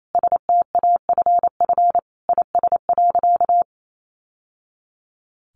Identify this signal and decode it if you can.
Morse code, heard as a keyed tone, about 28 wpm, 700 Hz. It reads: STAFF IS.
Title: Sentences from Top 1200 Words 28wpm